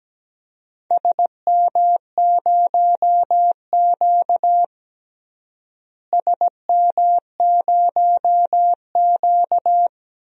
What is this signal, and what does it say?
Morse code, heard as a keyed tone, about 17 wpm, 700 Hz. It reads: SM0Q SM0Q
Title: DXCC-100 Call Signs - 4 Characters - Repeated 3x - 17wpm